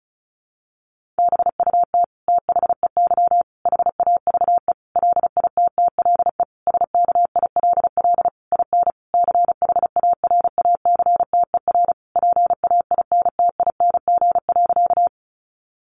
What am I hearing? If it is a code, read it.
Morse code, heard as a keyed tone, about 34 wpm, 700 Hz. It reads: BUT THEY HAVE LITTLE SKILL IN CHARACTER PAINTING.
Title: Sentences from Top 1200 Words 35wpm